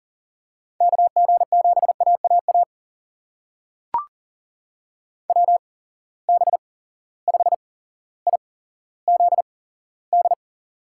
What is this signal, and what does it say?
Morse code, heard as a keyed tone, about 41 wpm, spaced as for 17 wpm, 700 Hz. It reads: KG7AAA WB5IZD
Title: USA Call Signs - 6 Characters - 8x Character Spacing - 40wpm